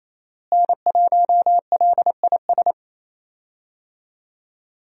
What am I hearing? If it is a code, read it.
Morse code, heard as a keyed tone, about 28 wpm, 700 Hz. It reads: N1LIS
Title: USA Call Signs - 5 Characters - Repeated 3x - 28wpm